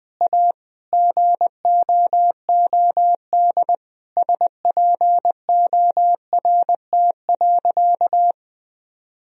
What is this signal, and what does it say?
Morse code, heard as a keyed tone, about 20 wpm, 700 Hz. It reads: A GOOD SPORT.
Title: Sentences from Top 1000 Words - Review Part 4 - 20wpm